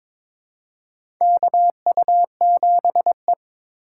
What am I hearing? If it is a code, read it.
Morse code, heard as a keyed tone, about 22 wpm, 700 Hz. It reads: KU7E
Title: USA Call Signs - 4 Characters - 4x Character Spacing - 22wpm